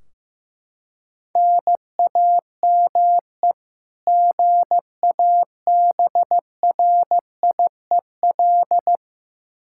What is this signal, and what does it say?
Morse code, heard as a keyed tone, about 15 wpm, 700 Hz. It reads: NAME GABRIEL
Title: K1USN SST QSO Practice 15wpm